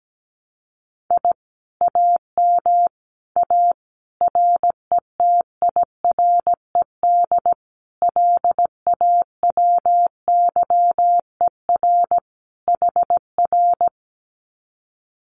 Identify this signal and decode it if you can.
Morse code, heard as a keyed tone, about 17 wpm, 700 Hz. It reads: I AM A RETIRED LAWYER HR
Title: Common QSO Phrases Encore 17wpm